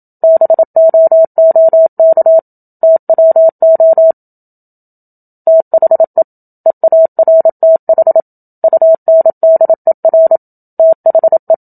Morse code, heard as a keyed tone, about 28 wpm, 650 Hz. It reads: BOOK TWO THE EARTH UNDER THE
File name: Book Two / 01-Chapter 1 - 27 WPM